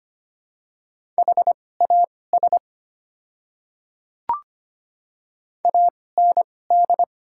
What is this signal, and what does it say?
Morse code, heard as a keyed tone, about 25 wpm, spaced as for 18 wpm, 690 Hz. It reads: HAS AND
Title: 2x Character Spacing - Top 100 Words - 25wpm